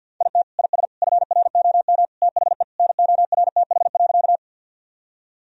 Morse code, heard as a keyed tone, about 50 wpm, 700 Hz. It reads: IT IS FROM THE NORTH.
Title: Sentences from Top 1000 Words - Review Part 1 - 50wpm